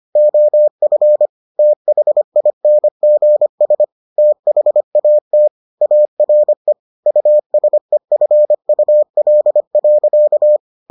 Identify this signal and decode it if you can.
Morse code, heard as a keyed tone, about 25 wpm, 600 Hz. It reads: OF THINGS THAT ARE USEFUL.